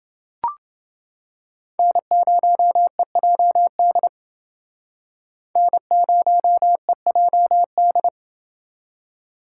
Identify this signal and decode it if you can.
Morse code, heard as a keyed tone, about 28 wpm, 700 Hz. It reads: N0EJD N0EJD